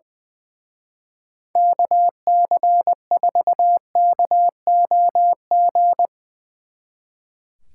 Morse code, heard as a keyed tone, about 21 wpm, 700 Hz. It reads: KC4KOG
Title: Speed Racing - USA 2x3 Call Signs 20wpm